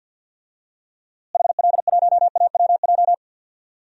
Morse code, heard as a keyed tone, about 47 wpm, 700 Hz. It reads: HL1AWJ
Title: DXCC-100 Call Signs - 6 Characters - Repeated 2x - 50wpm